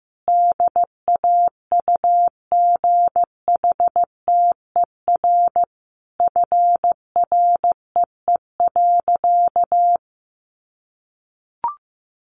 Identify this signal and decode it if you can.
Morse code, heard as a keyed tone, about 15 wpm, 700 Hz. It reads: DAUGHTER FREE.
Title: Sentences from Top 1500 Words 15wpm